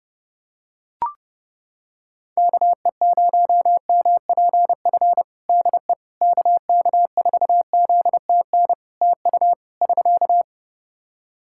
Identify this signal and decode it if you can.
Morse code, heard as a keyed tone, about 30 wpm, 700 Hz. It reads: KE0MPF DE KK4ZTN TU <SK>
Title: Practice QSO - Exchange 4 - Ending - 30wpm